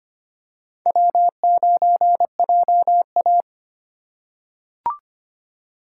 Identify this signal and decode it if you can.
Morse code, heard as a keyed tone, about 25 wpm, 700 Hz. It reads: W9JA
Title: USA Call Signs - 2x Character Spacing - 25wpm